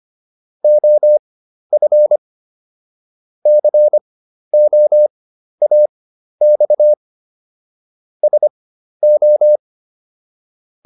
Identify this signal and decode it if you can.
Morse code, heard as a keyed tone, about 25 wpm, spaced as for 12 wpm, 600 Hz. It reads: OF COAX SO